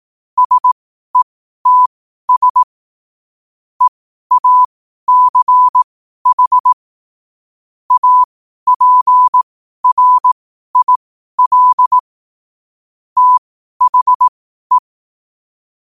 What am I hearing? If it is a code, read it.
Morse code, heard as a keyed tone, about 18 wpm, spaced as for 12 wpm, 1000 Hz. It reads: SETS EACH APRIL THE
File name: qotdfiles / 12wpm